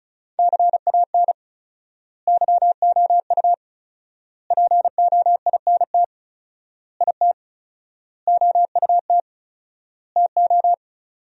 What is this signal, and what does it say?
Morse code, heard as a keyed tone, about 34 wpm, 700 Hz. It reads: CAN YOU POINT IT OUT TO
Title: 4x Word-Spacing - Sentences from Top 200 Words - 35wpm